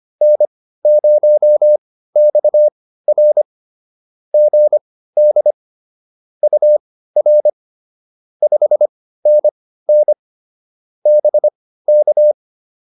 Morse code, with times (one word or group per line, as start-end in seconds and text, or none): 0.2-3.4 N0XR
4.3-5.5 GD
6.4-7.5 UR
8.4-10.1 5NN
11.1-12.3 BK